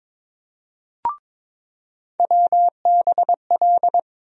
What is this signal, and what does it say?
Morse code, heard as a keyed tone, about 22 wpm, 700 Hz. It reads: WBL
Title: USA Call Sign Postfixes - 3 Letters 22wpm